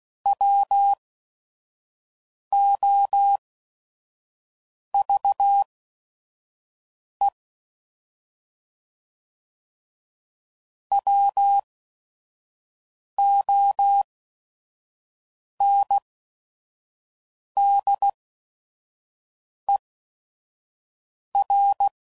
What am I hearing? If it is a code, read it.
Morse code, heard as a keyed tone, about 16 wpm, spaced as for 5 wpm, 800 Hz. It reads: WOVE WONDER
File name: Prefixes / By First Letter / Ws